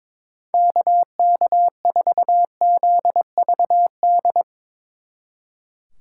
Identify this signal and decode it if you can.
Morse code, heard as a keyed tone, about 24 wpm, 700 Hz. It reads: KK4ZVD
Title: Speed Racing - USA 2x3 Call Signs 22wpm